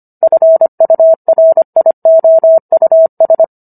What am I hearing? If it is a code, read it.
Morse code, heard as a keyed tone, about 25 wpm, 650 Hz. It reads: FURIOUS